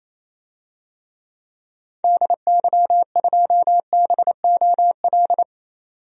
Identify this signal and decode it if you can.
Morse code, heard as a keyed tone, about 28 wpm, 700 Hz. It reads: DY2BOL
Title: DXCC-100 Call Signs - 6 Characters - 28wpm